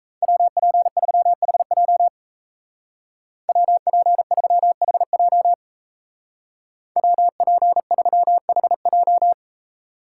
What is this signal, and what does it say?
Morse code, heard as a keyed tone, about 37 wpm, 700 Hz. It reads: WP3HJ WP3HJ WP3HJ